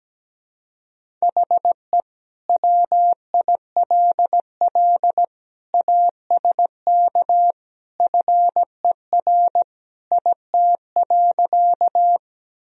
Morse code, heard as a keyed tone, about 17 wpm, 700 Hz. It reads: HE WILL ASK FER IT.
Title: Sentences from Top 200 Words and Abbreviations 17wpm